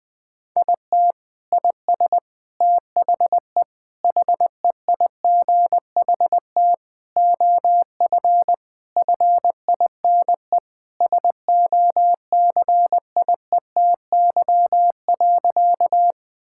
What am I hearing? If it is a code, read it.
Morse code, heard as a keyed tone, about 20 wpm, 700 Hz. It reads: IT IS THE HEIGHT OF FINE SOCIETY.
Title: Sentences from Top 1600 Words 20wpm